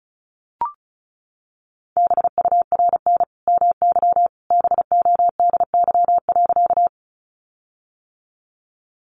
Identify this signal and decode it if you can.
Morse code, heard as a keyed tone, about 34 wpm, 700 Hz. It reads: BURN MY BODY.
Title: Sentences from Top 1300 Words 35wpm